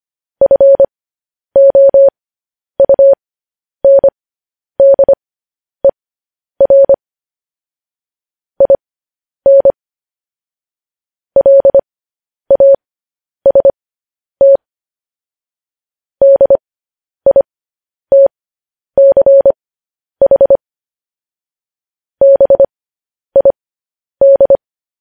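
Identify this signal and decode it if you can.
Morse code, heard as a keyed tone, about 25 wpm, spaced as for 10 wpm, 550 Hz. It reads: FOUNDER IN LAST DITCH BID
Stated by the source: Morse news headlines practice at 10.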